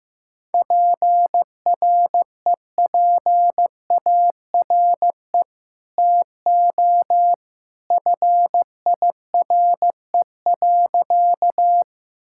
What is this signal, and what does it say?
Morse code, heard as a keyed tone, about 15 wpm, 700 Hz. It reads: PREPARE TO FIRE.
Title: Sentences from Top 1000 Words - Review Part 6 - 15wpm